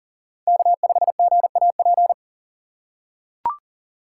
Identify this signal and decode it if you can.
Morse code, heard as a keyed tone, about 40 wpm, 700 Hz. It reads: K5GAP
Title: USA Call Signs - 5 Characters - 2x Character Spacing - 40wpm